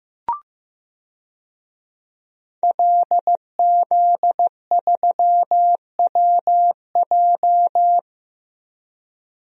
Morse code, heard as a keyed tone, about 15 wpm, 700 Hz. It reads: LZ3WJ
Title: DXCC-100 Call Signs - 5 Characters - Repeated 2x - 15wpm